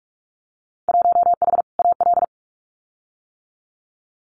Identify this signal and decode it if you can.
Morse code, heard as a keyed tone, about 46 wpm, 700 Hz. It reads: 1H AL
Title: Winter Field Day Exchanges 45wpm